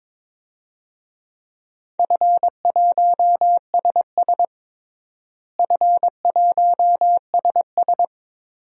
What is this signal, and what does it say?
Morse code, heard as a keyed tone, about 22 wpm, 700 Hz. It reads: F1SS F1SS
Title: DXCC-100 Call Signs - 4 Characters - Repeated 2x - 22wpm